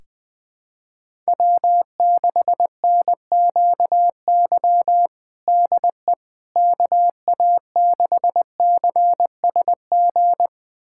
Morse code, heard as a keyed tone, about 21 wpm, 700 Hz. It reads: W6NQY DE KA6CSG
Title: USA Callsign DE Callsign 20wpm